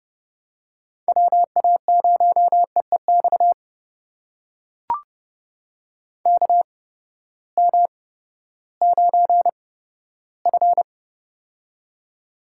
Morse code, heard as a keyed tone, about 32 wpm, spaced as for 15 wpm, 700 Hz. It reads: WA0EEX KM9F
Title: USA Call Signs - 8x Character Spacing - 30wpm